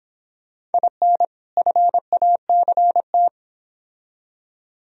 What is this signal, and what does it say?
Morse code, heard as a keyed tone, about 28 wpm, 700 Hz. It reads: IN FACT